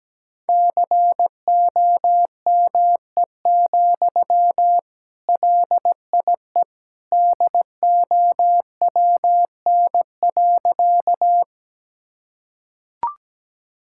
Morse code, heard as a keyed tone, about 17 wpm, 700 Hz. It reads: COME, LIE DOWN.
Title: Sentences from Top 1600 Words 17wpm